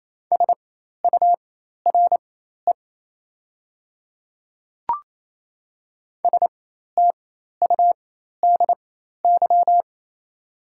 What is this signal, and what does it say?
Morse code, heard as a keyed tone, about 27 wpm, spaced as for 13 wpm, 700 Hz. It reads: SURE STUDY
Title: 4x Character Spacing - Top 300 Words - 28wpm